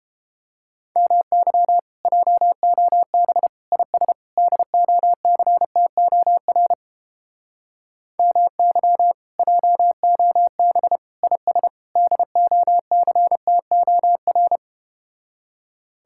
Morse code, heard as a keyed tone, about 32 wpm, 700 Hz. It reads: MY JOB IS DOCTOR MY JOB IS DOCTOR